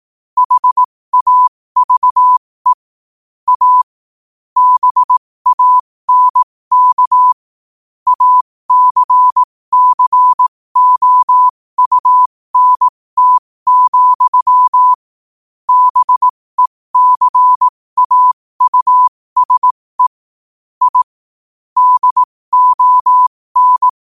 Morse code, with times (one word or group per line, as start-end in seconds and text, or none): 0.4-2.7 HAVE
3.5-3.8 A
4.6-7.3 BANK
8.1-14.9 ACCOUNT,
15.7-20.1 BECAUSE
20.8-21.0 I
21.8-23.9 DON